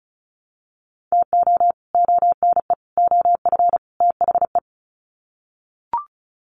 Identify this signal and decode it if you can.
Morse code, heard as a keyed tone, about 34 wpm, 700 Hz. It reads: TO ONE OF THE
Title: Sets of 4 Words 35wpm